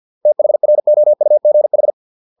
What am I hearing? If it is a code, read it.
Morse code, heard as a keyed tone, about 50 wpm, 600 Hz. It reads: THROUGH